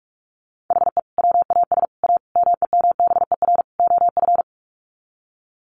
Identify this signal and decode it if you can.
Morse code, heard as a keyed tone, about 44 wpm, 700 Hz. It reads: HE WAS A MEMBER OF
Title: Sets of 5 Words Encore2 45wpm